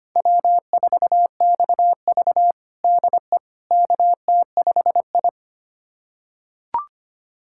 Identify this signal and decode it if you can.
Morse code, heard as a keyed tone, about 25 wpm, 700 Hz. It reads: W4XV DE KT5I
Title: USA Callsign DE Callsign - 4 Characters - 25wpm